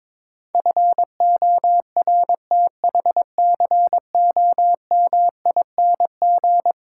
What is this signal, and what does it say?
Morse code, heard as a keyed tone, about 22 wpm, 700 Hz. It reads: FORTHCOMING
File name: practice / 22wpm / Eleven Letter Words - 22wpm